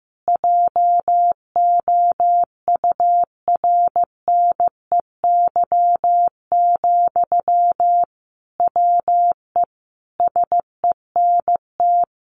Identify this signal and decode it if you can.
Morse code, heard as a keyed tone, about 15 wpm, 700 Hz. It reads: JOURNEY, WE SENT
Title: A Sound Of Thunder 15wpm